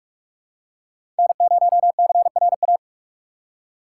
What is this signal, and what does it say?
Morse code, heard as a keyed tone, about 43 wpm, 700 Hz. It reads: N0CRA
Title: USA Call Signs - 5 Characters - 6x Character Spacing - 45wpm